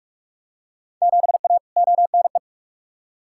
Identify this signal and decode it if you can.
Morse code, heard as a keyed tone, about 43 wpm, 700 Hz. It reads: 7A ONE
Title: Field Day Exchanges Encore - 45wpm